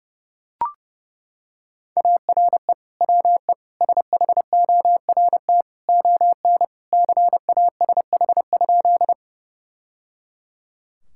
Morse code, heard as a keyed tone, about 30 wpm, 700 Hz. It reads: ARE WE SHORT ON CASH?
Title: Sentences from Top 1000 Words - Review Part 5 - 30wpm